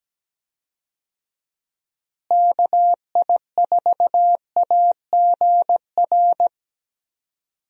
Morse code, heard as a keyed tone, about 17 wpm, 700 Hz. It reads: KI4AGR
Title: USA Call Signs - 6 Characters - Repeated 2x - 17wpm